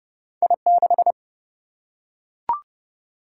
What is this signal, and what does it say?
Morse code, heard as a keyed tone, about 30 wpm, 700 Hz. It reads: I6
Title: Single-Letter Single-Number 30wpm